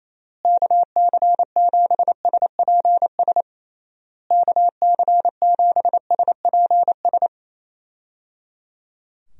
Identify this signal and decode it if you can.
Morse code, heard as a keyed tone, about 28 wpm, 700 Hz. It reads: KC7SPS KC7SPS
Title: USA Call Signs - 6 Characters - Repeated 3x - 28wpm